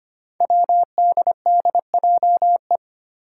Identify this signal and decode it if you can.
Morse code, heard as a keyed tone, about 25 wpm, 700 Hz. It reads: WDDJE
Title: Five Challenging Letter-Number Combinations 25wpm